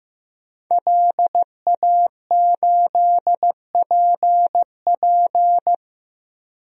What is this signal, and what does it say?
Morse code, heard as a keyed tone, about 15 wpm, 700 Hz. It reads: LA8PP
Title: DXCC-100 Call Signs - 5 Characters - 15wpm